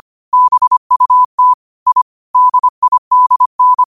Morse code, heard as a keyed tone, about 25 wpm, 1000 Hz. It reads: BUT I DIDN
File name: qotdfiles / 25wpm